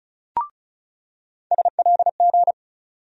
Morse code, heard as a keyed tone, about 35 wpm, 700 Hz. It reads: SLG
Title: USA Call Sign Postfixes - 3 Letters 35wpm